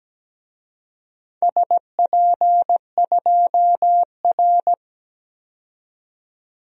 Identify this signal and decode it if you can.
Morse code, heard as a keyed tone, about 17 wpm, 700 Hz. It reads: SP2R
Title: DXCC-100 Call Signs - 4 Characters - 17wpm